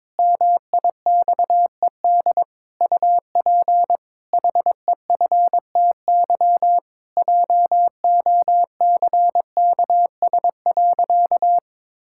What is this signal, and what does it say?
Morse code, heard as a keyed tone, about 22 wpm, 700 Hz. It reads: MIXED UP HEFTY JOCKS.
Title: Pangram Challenge v2 - 22wpm